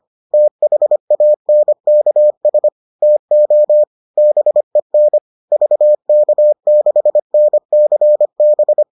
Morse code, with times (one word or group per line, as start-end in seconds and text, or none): 0.3-2.7 THANKS
3.0-3.8 TO
4.2-5.2 BEN
5.5-8.8 VK6NCB